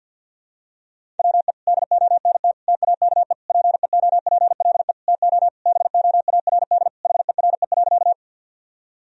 Morse code, heard as a keyed tone, about 50 wpm, 700 Hz. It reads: WE DONT TAKE PEOPLE TO BOARD HERE.
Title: Sentences from Top 1000 Words - Review Part 4 - 50wpm